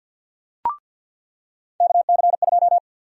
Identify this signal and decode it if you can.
Morse code, heard as a keyed tone, about 46 wpm, 700 Hz. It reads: KC2
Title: USA Call Sign Prefixes - Repeated 2x - 50wpm